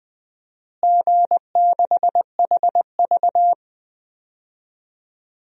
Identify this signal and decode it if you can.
Morse code, heard as a keyed tone, about 21 wpm, 700 Hz. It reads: G6HV
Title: DXCC-100 Call Signs - 4 Characters - Repeated 3x - 20wpm